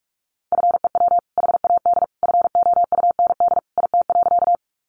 Morse code, heard as a keyed tone, about 44 wpm, 700 Hz. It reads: FEW HAD FOUND IT.
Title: Sentences from Top 300 Words 45wpm